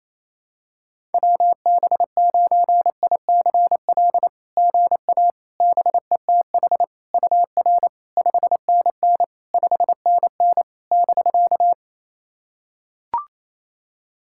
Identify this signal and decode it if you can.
Morse code, heard as a keyed tone, about 28 wpm, 660 Hz. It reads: WB9ICL GA BETH UR 5NN 5NN <BK>
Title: SOTA QSO Practice 28wpm